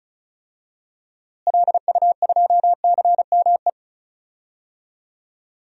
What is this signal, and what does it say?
Morse code, heard as a keyed tone, about 34 wpm, 700 Hz. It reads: LU2CME
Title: DXCC-100 Call Signs - 6 Characters - Repeated 3x - 35wpm